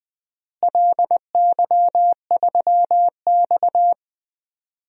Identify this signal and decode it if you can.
Morse code, heard as a keyed tone, about 20 wpm, 700 Hz. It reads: LY3X
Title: DXCC-100 Call Signs - 4 Characters - Repeated 3x - 20wpm